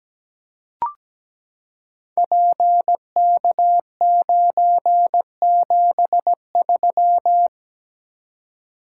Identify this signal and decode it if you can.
Morse code, heard as a keyed tone, about 17 wpm, 700 Hz. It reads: PK973
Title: Code Groups - 5 Letter:Numbers - 17wpm